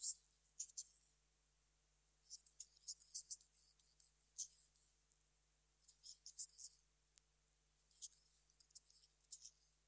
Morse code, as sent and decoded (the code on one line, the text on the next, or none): none